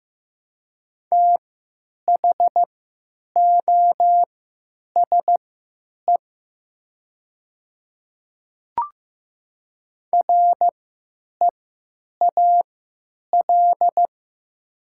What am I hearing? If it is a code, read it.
Morse code, heard as a keyed tone, about 15 wpm, spaced as for 9 wpm, 700 Hz. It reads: THOSE REAL
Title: 3x Character Spacing - Top 300 Words - 15wpm